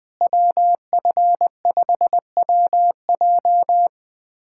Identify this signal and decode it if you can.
Morse code, heard as a keyed tone, about 21 wpm, 700 Hz. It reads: WF5WJ
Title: K1USN SST QSO Practice - 20wpm